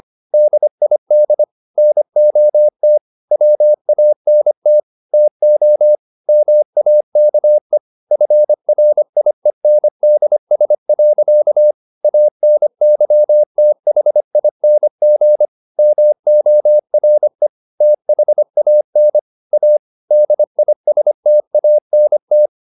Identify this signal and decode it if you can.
Morse code, heard as a keyed tone, about 25 wpm, 600 Hz. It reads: DID NOT WANT TO MAKE FRIENDS. ANYTHING MORE THAN A DISTANT